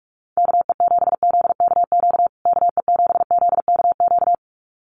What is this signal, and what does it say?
Morse code, heard as a keyed tone, about 44 wpm, 700 Hz. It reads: KE7ZKQ KE7ZKQ
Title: USA Call Signs Repeated 2x 45wpm